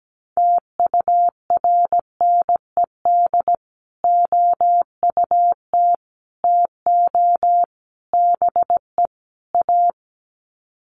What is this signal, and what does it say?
Morse code, heard as a keyed tone, about 17 wpm, 700 Hz. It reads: TURNED OUT TO BE A
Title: Sets of 5 Words 17wpm